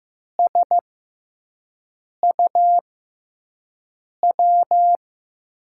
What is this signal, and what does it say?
Morse code, heard as a keyed tone, about 15 wpm, spaced as for 5 wpm, 700 Hz. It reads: SUW
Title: USA Call Signs - 6x Character Spacing - 15wpm